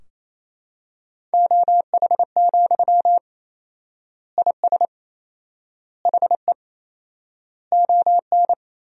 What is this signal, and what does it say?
Morse code, heard as a keyed tone, about 28 wpm, 700 Hz. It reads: OH, IS HE ON
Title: 4x Word-Spacing - Sentences from Top 500 Words - 28wpm